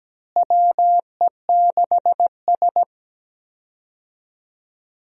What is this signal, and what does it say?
Morse code, heard as a keyed tone, about 17 wpm, 700 Hz. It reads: WE6S
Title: USA Call Signs - 4 Characters - Repeated 3x - 17wpm